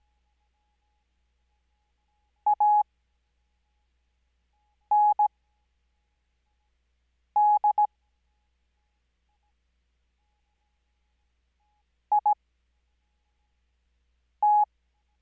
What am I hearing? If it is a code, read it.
Morse code, heard as a keyed tone, about 17 wpm, spaced as for 4 wpm, 840 Hz. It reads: AND IT
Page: Farnsworth Spacing: 15WPM Character speed Spaced at 5WPM.